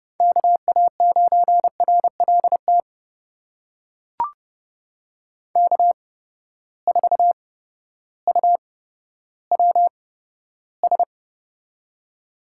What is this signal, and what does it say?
Morse code, heard as a keyed tone, about 30 wpm, spaced as for 14 wpm, 700 Hz. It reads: KA9RLT K4UWS
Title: USA Call Signs - 8x Character Spacing - 30wpm